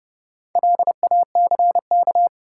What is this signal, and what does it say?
Morse code, heard as a keyed tone, about 31 wpm, 700 Hz. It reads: LACK
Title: Sets of 3 Words - New Word Drill - 30wpm